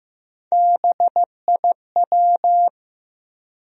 Morse code, heard as a keyed tone, about 15 wpm, 700 Hz. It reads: BIW
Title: USA Call Sign Postfixes - 3 Letters 15wpm